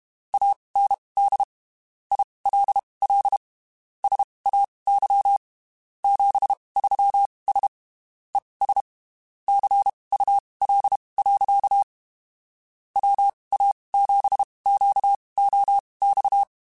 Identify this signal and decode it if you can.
Morse code, heard as a keyed tone, about 32 wpm, spaced as for 23 wpm, 800 Hz. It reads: AND ILL SAY 73S ES CUL. WA7QOX
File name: (spaced) / QSOs Spaced-mp3s / WA7QOX 2 @30 sp